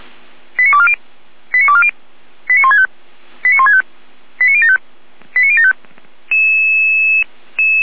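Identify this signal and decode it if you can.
Five-tone selective call (ZVEI), heard as quick runs of five tones, tones of about 70 ms. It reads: 89240 89240 89165 89165 89075 89075
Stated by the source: SDR-received ZVEI five-tone selective call.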